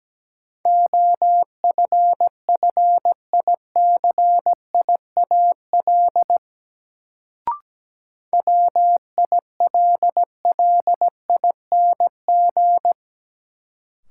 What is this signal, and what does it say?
Morse code, heard as a keyed tone, about 17 wpm, 700 Hz. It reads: OFFICIAL WILLING
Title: Sentences from Top 1000 Words - New Word Drill - 17wpm